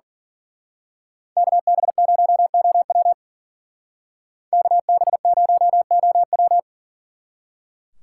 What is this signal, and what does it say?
Morse code, heard as a keyed tone, about 42 wpm, 700 Hz. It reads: KB0OW KB0OW
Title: Speed Racing - USA 2x2 Call Signs 40wpm